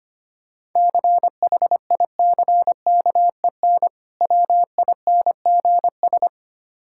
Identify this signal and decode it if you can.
Morse code, heard as a keyed tone, about 25 wpm, 700 Hz. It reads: CHICKEN WINGS
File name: practice / 25wpm / Difficult Words with Hints - 25wpm